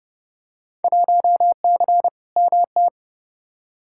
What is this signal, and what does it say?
Morse code, heard as a keyed tone, about 30 wpm, 700 Hz. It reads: 1C MT
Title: Field Day Exchanges Encore - 30wpm